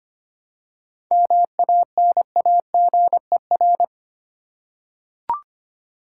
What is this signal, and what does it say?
Morse code, heard as a keyed tone, about 25 wpm, 700 Hz. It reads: MANAGER
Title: Practice QSO - Exchange 3 - More Info - 25wpm